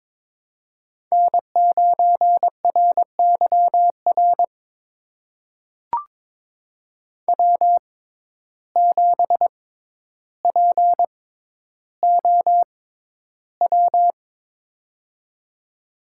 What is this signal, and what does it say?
Morse code, heard as a keyed tone, about 22 wpm, spaced as for 11 wpm, 700 Hz. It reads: N9RYR W7POW
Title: USA Call Signs - 5 Characters - 6x Character Spacing - 22wpm